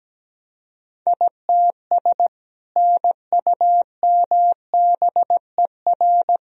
Morse code, heard as a keyed tone, about 17 wpm, 700 Hz. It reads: ITS NUMBER
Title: Sentences from Top 1000 Words - Review Part 1 - 17wpm